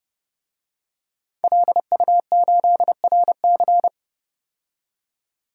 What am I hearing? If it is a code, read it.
Morse code, heard as a keyed tone, about 32 wpm, 700 Hz. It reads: LU8RC
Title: DXCC-100 Call Signs - 5 Characters - 30wpm